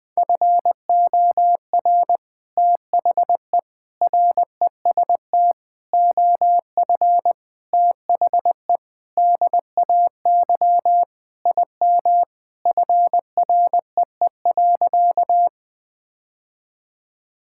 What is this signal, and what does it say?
Morse code, heard as a keyed tone, about 21 wpm, 700 Hz. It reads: FOR THE REST OF THE DAY IM FREE.
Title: Sentences with 3-Word Phrases - 20wpm